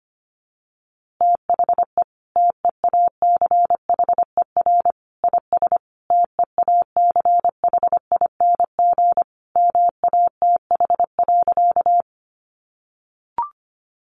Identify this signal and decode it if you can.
Morse code, heard as a keyed tone, about 25 wpm, 700 Hz. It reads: THE TEACHER IS TEACHING MATH.
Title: Easy Phrases 25wpm